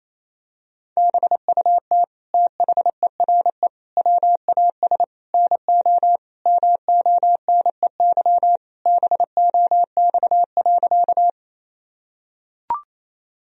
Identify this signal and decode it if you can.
Morse code, heard as a keyed tone, about 28 wpm, 700 Hz. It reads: BUT THERE WAS NO MONEY BOX.
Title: Sentences from Top 1000 Words - Review Part 2 - 28wpm